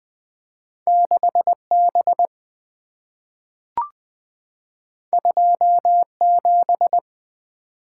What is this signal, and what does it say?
Morse code, heard as a keyed tone, about 20 wpm, 690 Hz. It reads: 6B 27
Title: Two Letter-Number Combinations - 20wpm